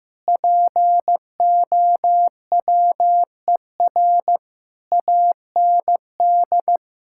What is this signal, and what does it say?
Morse code, heard as a keyed tone, about 16 wpm, 700 Hz. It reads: POWER AND
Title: Sentences from Top 1000 Words - Review Part 5 - 15wpm